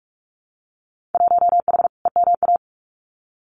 Morse code, heard as a keyed tone, about 46 wpm, 700 Hz. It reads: 1H EMA
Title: Winter Field Day Exchanges 45wpm